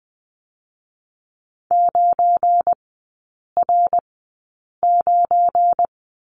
Morse code, heard as a keyed tone, about 20 wpm, spaced as for 8 wpm, 700 Hz. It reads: 9R9
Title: 172 Lesson 28e - Number 9 - Three Characters 20wpm